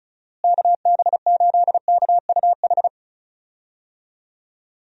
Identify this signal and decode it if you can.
Morse code, heard as a keyed tone, about 34 wpm, 700 Hz. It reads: KB8KUH
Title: USA Call Signs - 6 Characters - Repeated 2x - 35wpm